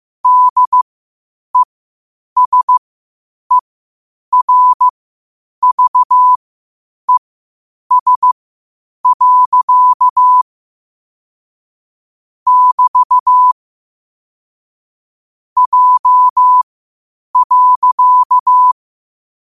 Morse code, heard as a keyed tone, about 15 wpm, spaced as for 8 wpm, 1000 Hz. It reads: DESERVES. <BT> J.
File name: qotdfiles / 8wpm